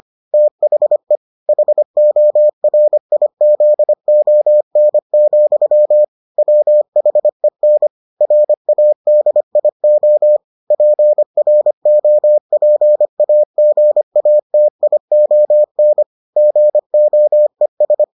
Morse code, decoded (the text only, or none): THE HORIZON, WHEN RADIO PROPAGATION GOES